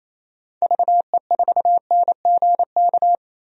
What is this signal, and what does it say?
Morse code, heard as a keyed tone, about 27 wpm, 700 Hz. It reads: VE4NGK
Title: DXCC-100 Call Signs - 6 Characters - 28wpm